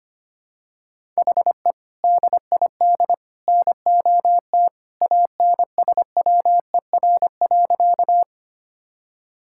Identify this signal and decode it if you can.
Morse code, heard as a keyed tone, about 25 wpm, 700 Hz. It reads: HE DID NOT ANSWER.
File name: Sentences with 3-Word Phrases - 25wpm